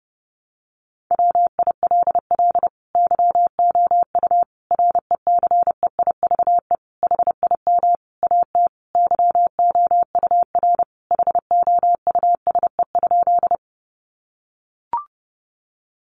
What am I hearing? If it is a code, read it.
Morse code, heard as a keyed tone, about 30 wpm, 700 Hz. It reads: WILL YOU RECEIVE HIM AT YOUR HOUSE?
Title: Sentences from Top 1000 Words 30wpm